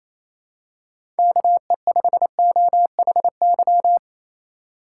Morse code, heard as a keyed tone, about 28 wpm, 700 Hz. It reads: KE5OHY
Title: USA Call Signs - 2x Character Spacing - 28wpm